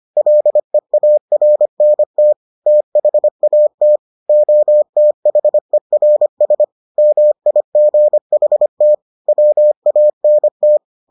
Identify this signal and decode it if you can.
Morse code, heard as a keyed tone, about 25 wpm, 600 Hz. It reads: LEARNT THAT OTHERS MIGHT WANT